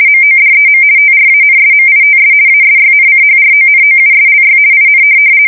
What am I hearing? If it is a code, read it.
RTTY, about 50 baud. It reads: MODEM TRANSMITS ASYNCHRONOUSLY IN
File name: fsk170-50B / teletype-corp